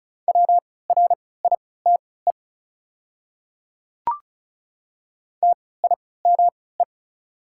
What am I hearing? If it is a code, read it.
Morse code, heard as a keyed tone, about 36 wpm, spaced as for 20 wpm, 700 Hz. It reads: WRITE TIME
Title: 3x Character Spacing - Top 200 Words - 35wpm